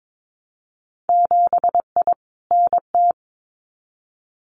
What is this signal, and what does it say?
Morse code, heard as a keyed tone, about 22 wpm, 700 Hz. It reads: 7I NT
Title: Winter Field Day Exchanges Encore 22wpm